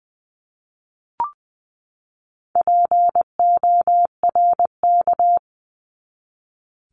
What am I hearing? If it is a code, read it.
Morse code, heard as a keyed tone, about 20 wpm, 700 Hz. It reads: PORK
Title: 260 Lesson 42e - Words 20-20wpm